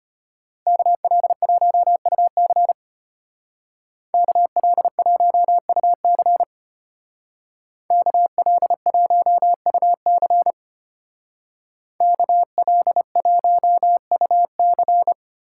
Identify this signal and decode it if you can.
Morse code, heard as a keyed tone, about 31 wpm, 700 Hz. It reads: KL1UC KL1UC KL1UC KL1UC
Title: Speed Racing - USA 2x2 Call Signs 25wpm